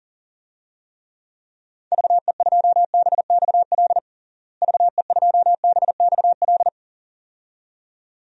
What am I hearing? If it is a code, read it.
Morse code, heard as a keyed tone, about 40 wpm, 700 Hz. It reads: VE2BXL VE2BXL
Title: DXCC-100 Call Signs - 6 Characters - Repeated 2x - 40wpm